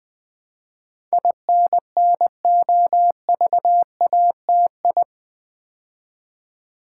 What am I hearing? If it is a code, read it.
Morse code, heard as a keyed tone, about 20 wpm, 700 Hz. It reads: INNOVATI
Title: Build A Word - 4 Letters At A Time - 10-Letter Words - 20wpm